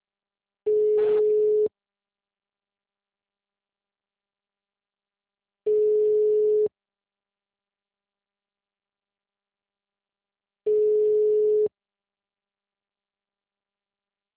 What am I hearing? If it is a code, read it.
Ringback tone (Europe, 425 Hz).
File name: out- -2020